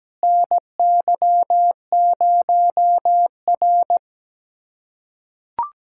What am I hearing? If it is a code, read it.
Morse code, heard as a keyed tone, about 17 wpm, 700 Hz. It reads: NY0R
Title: USA Callsign DE Callsign - 4 Characters - 17wpm